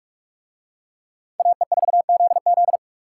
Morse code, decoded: AE4ZZ

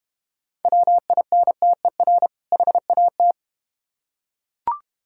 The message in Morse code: WINTER HAT

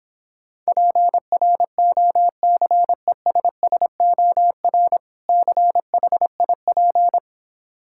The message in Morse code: PROCESSOR CHIP

